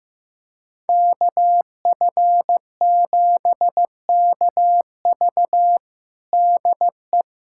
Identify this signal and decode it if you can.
Morse code, heard as a keyed tone, about 16 wpm, 700 Hz. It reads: KF7KV DE
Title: USA Callsign DE Callsign - 5 Characters - 15wpm